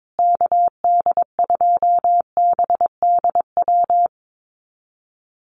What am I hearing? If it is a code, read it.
Morse code, heard as a keyed tone, about 22 wpm, 700 Hz. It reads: KD2BDW
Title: USA Call Signs Repeated 3x 22wpm